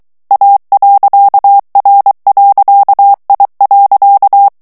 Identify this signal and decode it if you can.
Morse code, heard as a keyed tone, about 23 wpm, 800 Hz. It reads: A.R.I.
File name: ITALIEN texte clair 2 vitesse 22 mots minute